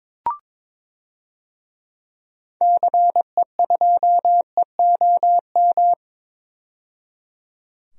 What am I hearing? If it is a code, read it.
Morse code, heard as a keyed tone, about 22 wpm, 700 Hz. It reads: CE2EOM
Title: DXCC-100 Call Signs - 6 Characters - 22wpm